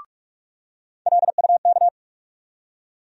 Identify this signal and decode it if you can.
Morse code, heard as a keyed tone, about 44 wpm, 700 Hz. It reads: LUK
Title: USA Call Sign Postfixes - 3 Letters 45wpm